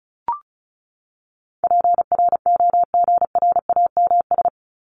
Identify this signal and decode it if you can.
Morse code, heard as a keyed tone, about 34 wpm, 700 Hz. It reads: PROGRAMS
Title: ES:S Suffix - Top 100 Words with Base Words 35wpm